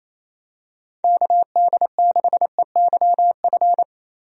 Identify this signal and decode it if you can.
Morse code, heard as a keyed tone, about 28 wpm, 700 Hz. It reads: KD6EYF
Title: USA Call Signs - 6 Characters - 28wpm